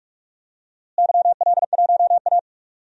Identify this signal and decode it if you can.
Morse code, heard as a keyed tone, about 42 wpm, 700 Hz. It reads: YL1A
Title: DXCC-100 Call Signs - 4 Characters - Repeated 3x - 45wpm